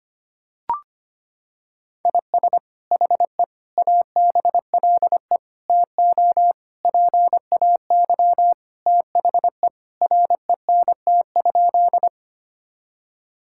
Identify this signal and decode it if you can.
Morse code, heard as a keyed tone, about 25 wpm, 700 Hz. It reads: IS HE ABLE TO PAY THE RENT?
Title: Sentences from Top 1000 Words - Review Part 5 - 25wpm